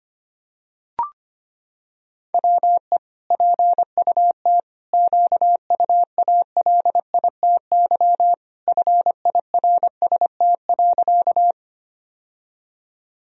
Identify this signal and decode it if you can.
Morse code, heard as a keyed tone, about 25 wpm, 700 Hz. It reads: WE PUT QUALITY FIRST.
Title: Sentences from Top 1000 Words - Review Part 4 - 25wpm